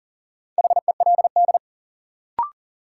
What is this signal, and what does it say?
Morse code, heard as a keyed tone, about 40 wpm, 700 Hz. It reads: HELD
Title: Sentences from Top 1600 Words - New Word Drill - 40wpm